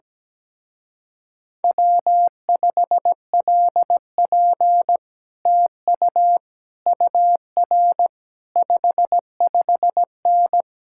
Morse code, heard as a keyed tone, about 17 wpm, 700 Hz. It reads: W5LP TU UR 55N
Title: Speed-Racing Common QSO Phrases - 17wpm